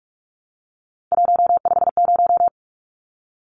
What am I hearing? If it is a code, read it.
Morse code, heard as a keyed tone, about 44 wpm, 700 Hz. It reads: 150
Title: Numbers - 3-Digit 45wpm